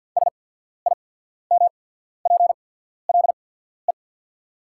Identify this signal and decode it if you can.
Morse code, heard as a keyed tone, about 48 wpm, spaced as for 14 wpm, 690 Hz. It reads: SIMPLE
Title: 8x Character Spacing - Top 500 Words - 50wpm